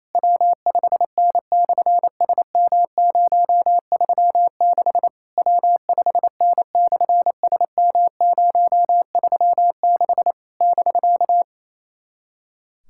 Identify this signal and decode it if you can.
Morse code, heard as a keyed tone, about 29 wpm, 700 Hz. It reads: W5N/SM036 W5N/SM036 <BK>
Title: SOTA QSO Practice 28wpm